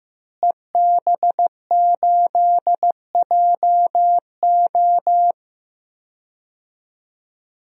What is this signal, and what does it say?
Morse code, heard as a keyed tone, about 15 wpm, 700 Hz. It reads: EB8JO
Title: DXCC-100 Call Signs - 5 Characters - Repeated 2x - 15wpm